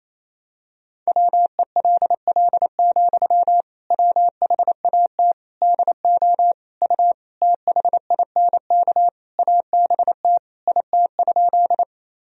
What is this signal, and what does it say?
Morse code, heard as a keyed tone, about 28 wpm, 700 Hz. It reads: WELL, WHAT DO U THINK ABT IT?
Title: Sentences from Top 200 Words and Abbreviations 28wpm